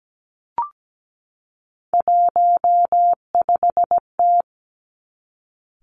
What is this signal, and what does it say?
Morse code, heard as a keyed tone, about 17 wpm, 700 Hz. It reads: 15T
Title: Three Challenging Letter-Number Combinations 17wpm